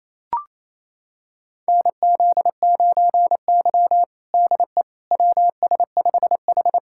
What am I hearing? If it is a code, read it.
Morse code, heard as a keyed tone, about 27 wpm, 700 Hz. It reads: NZ9Y DE WS5H
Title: USA Callsign DE Callsign - 4 Characters - 28wpm